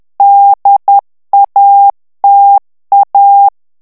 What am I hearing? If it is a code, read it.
Morse code, heard as a keyed tone, about 10 wpm, 800 Hz. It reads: DATA
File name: ANGLAIS texte clair 3 vitesse 10 mots minute